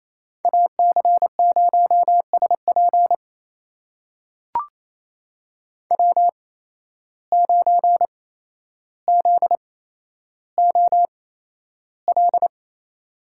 Morse code, heard as a keyed tone, about 28 wpm, spaced as for 12 wpm, 700 Hz. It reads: AC0SP W9ZOL
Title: USA Call Signs - 5 Characters - 8x Character Spacing - 28wpm